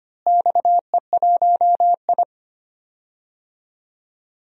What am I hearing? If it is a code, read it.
Morse code, heard as a keyed tone, about 25 wpm, 700 Hz. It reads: XE1I
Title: DXCC-100 Call Signs - 4 Characters - Repeated 3x - 25wpm